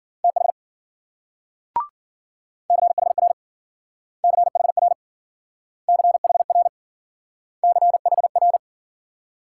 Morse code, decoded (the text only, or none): TH CHR CHR CHR CHR